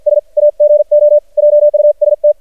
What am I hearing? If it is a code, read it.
Morse code, heard as a keyed tone, about 50 wpm, 590 Hz. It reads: S AGO JUST
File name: archived / CRSnet